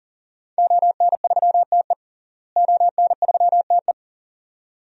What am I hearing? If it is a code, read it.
Morse code, heard as a keyed tone, about 40 wpm, 700 Hz. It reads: ON3TE ON3TE
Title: DXCC-100 Call Signs - 5 Characters - Repeated 2x - 40wpm